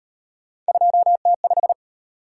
Morse code, heard as a keyed tone, about 38 wpm, 700 Hz. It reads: 2T5